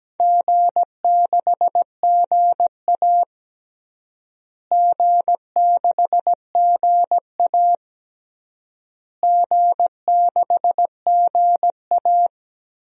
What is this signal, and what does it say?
Morse code, heard as a keyed tone, about 17 wpm, 700 Hz. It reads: G6GA G6GA G6GA
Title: DXCC-100 Call Signs - 4 Characters - Repeated 3x - 17wpm